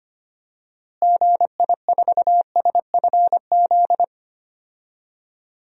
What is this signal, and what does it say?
Morse code, heard as a keyed tone, about 25 wpm, 700 Hz. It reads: GI4SFZ
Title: DXCC-100 Call Signs - 6 Characters - Repeated 2x - 25wpm